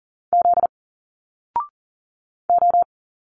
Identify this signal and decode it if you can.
Morse code, heard as a keyed tone, about 40 wpm, 700 Hz. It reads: Z O
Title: Single Letters 40wpm